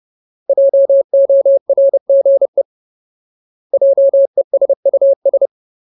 Morse code, heard as a keyed tone, about 30 wpm, 550 Hz. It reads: JORGE JESUS
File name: Ultimate Word List / 30WPM Ultimate